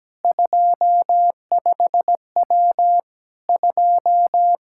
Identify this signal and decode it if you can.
Morse code, heard as a keyed tone, about 17 wpm, 700 Hz. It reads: 25W 2
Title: K1USN SST QSO Practice 17wpm